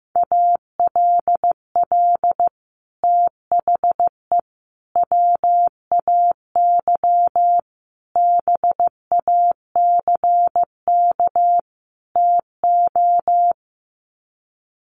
Morse code, heard as a keyed tone, about 15 wpm, 700 Hz. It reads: ALL THE WAY BACK TO
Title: Sets of 5 Words Encore 15wpm